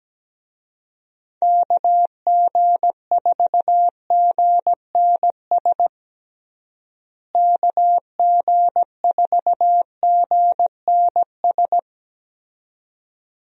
Morse code, heard as a keyed tone, about 17 wpm, 700 Hz. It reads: KG4GNS KG4GNS
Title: USA Call Signs - 6 Characters - Repeated 2x - 17wpm